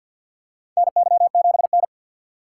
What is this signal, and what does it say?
Morse code, heard as a keyed tone, about 52 wpm, 700 Hz. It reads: NY7N
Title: K1USN SST QSO Practice - 50wpm